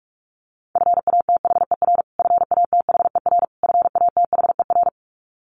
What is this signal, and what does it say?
Morse code, heard as a keyed tone, about 44 wpm, 700 Hz. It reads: FATHER FATHER FATHER
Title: Top 500 Words Repeated 45wpm